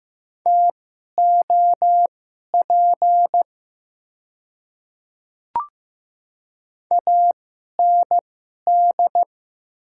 Morse code, heard as a keyed tone, about 15 wpm, spaced as for 11 wpm, 700 Hz. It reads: TOP AND
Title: 2x Character Spacing - Top 500 Words - 15wpm